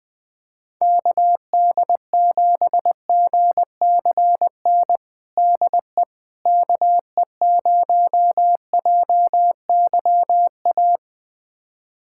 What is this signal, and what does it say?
Morse code, heard as a keyed tone, about 21 wpm, 700 Hz. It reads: KD7GCN DE KE0JYA
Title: USA Callsign DE Callsign 20wpm